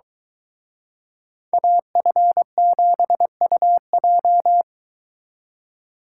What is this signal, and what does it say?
Morse code, heard as a keyed tone, about 22 wpm, 700 Hz. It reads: AF7UJ